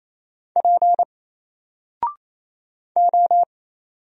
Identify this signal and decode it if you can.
Morse code, heard as a keyed tone, about 28 wpm, 700 Hz. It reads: P O
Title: Single Character (Letter:Number:Punctuation:Prosign) Alt 28wpm